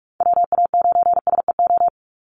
Morse code, heard as a keyed tone, about 44 wpm, 700 Hz. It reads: WA9SEO
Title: Call Signs - 45wpm